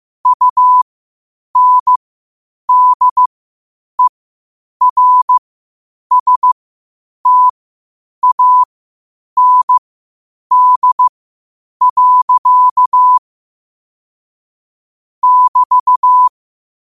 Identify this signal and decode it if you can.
Morse code, heard as a keyed tone, about 15 wpm, spaced as for 8 wpm, 1000 Hz. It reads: UNDERSTAND. <BT>
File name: qotdfiles / 8wpm